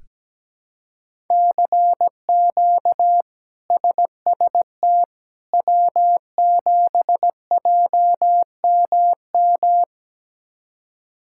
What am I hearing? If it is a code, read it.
Morse code, heard as a keyed tone, about 17 wpm, 700 Hz. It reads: CQ SST W7JMM
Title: K1USN SST QSO Practice - 17wpm